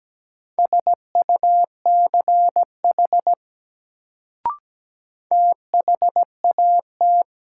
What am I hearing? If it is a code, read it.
Morse code, heard as a keyed tone, about 17 wpm, 700 Hz. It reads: SUCH THAT
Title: Sets of 2 Words - New Word Drill - 17wpm